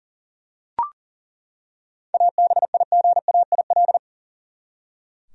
Morse code, heard as a keyed tone, about 40 wpm, 700 Hz. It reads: ABIGAIL
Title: USA Common Names - 40wpm